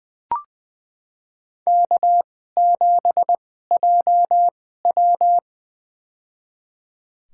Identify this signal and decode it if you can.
Morse code, heard as a keyed tone, about 20 wpm, spaced as for 15 wpm, 700 Hz. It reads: K7JW
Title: USA Call Signs - 4 Characters - 2x Character Spacing - 20wpm